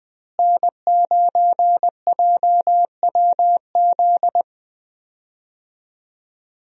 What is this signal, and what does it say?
Morse code, heard as a keyed tone, about 20 wpm, 700 Hz. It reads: N9JWZ
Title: USA Call Signs - 5 Characters - Repeated 2x - 20wpm